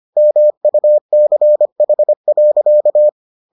Morse code, heard as a keyed tone, about 25 wpm, 600 Hz. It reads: MUCH.